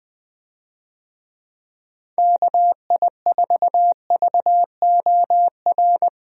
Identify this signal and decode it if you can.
Morse code, heard as a keyed tone, about 20 wpm, 700 Hz. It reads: KI4VOR
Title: USA Call Signs - 6 Characters - 20wpm